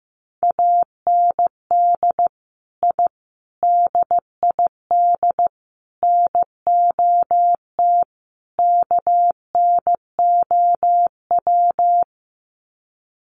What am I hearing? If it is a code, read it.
Morse code, heard as a keyed tone, about 15 wpm, 700 Hz. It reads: AND I DID NOT KNOW
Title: Sets of 5 Words Encore 15wpm